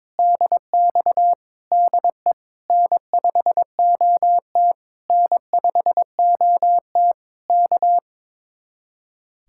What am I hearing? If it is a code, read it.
Morse code, heard as a keyed tone, about 22 wpm, 700 Hz. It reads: DX DE N5OT N5OT K
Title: Speed-Racing Common QSO Phrases - 22wpm